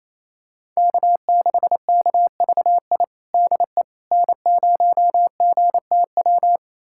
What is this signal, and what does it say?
Morse code, heard as a keyed tone, about 28 wpm, 700 Hz. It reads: K6KVI DE N0GTW
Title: USA Callsign DE Callsign - 5 Characters - 28wpm